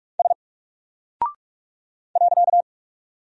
Morse code, heard as a keyed tone, about 46 wpm, 700 Hz. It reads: S .
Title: Single Character (Letter:Number:Punctuation:Prosign) 45wpm